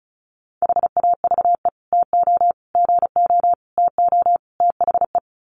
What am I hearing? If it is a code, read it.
Morse code, heard as a keyed tone, about 34 wpm, 700 Hz. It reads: HAVE TO GO TO THE
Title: Sets of 5 Words Encore 35wpm